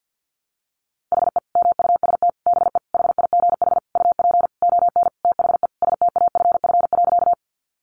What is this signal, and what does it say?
Morse code, heard as a keyed tone, about 50 wpm, 700 Hz. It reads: HE MUST BE HIGH UP ON THE STAFF.
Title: Sentences from Top 1200 Words 50wpm